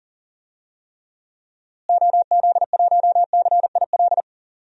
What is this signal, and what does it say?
Morse code, heard as a keyed tone, about 40 wpm, 700 Hz. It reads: OZ1CIL
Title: DXCC-100 Call Signs - 6 Characters - Repeated 2x - 40wpm